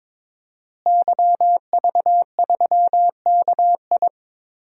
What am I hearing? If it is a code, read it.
Morse code, heard as a keyed tone, about 22 wpm, 700 Hz. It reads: YV3KI
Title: DXCC-100 Call Signs - 5 Characters - Repeated 2x - 22wpm